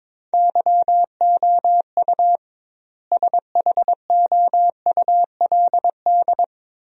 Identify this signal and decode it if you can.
Morse code, heard as a keyed tone, about 22 wpm, 700 Hz. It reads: YOU SHOULD
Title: 2x Word-Spacing - Sentences from Top 300 Words - 22wpm